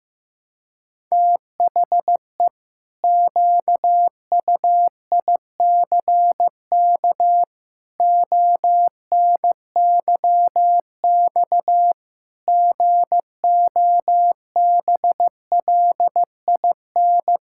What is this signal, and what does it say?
Morse code, heard as a keyed tone, about 15 wpm, 700 Hz. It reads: THE QUICK ONYX GOBLIN
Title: Pangram Challenge v2 - 15wpm